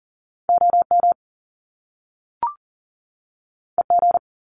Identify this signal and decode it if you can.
Morse code, heard as a keyed tone, about 41 wpm, 700 Hz. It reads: OM EG
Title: Top Two Letter Combinations in Words 40wpm